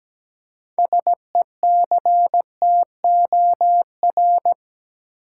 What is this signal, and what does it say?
Morse code, heard as a keyed tone, about 17 wpm, 700 Hz. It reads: SECTOR
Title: Six Letter Words - 17wpm